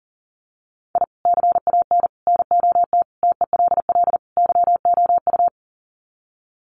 Morse code, heard as a keyed tone, about 40 wpm, 700 Hz. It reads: I CAN NOT TELL YOU
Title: Sets of 5 Words 40wpm